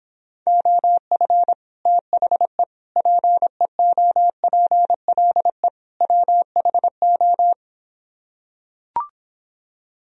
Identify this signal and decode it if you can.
Morse code, heard as a keyed tone, about 26 wpm, 700 Hz. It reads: OF THE PEOPLE WHO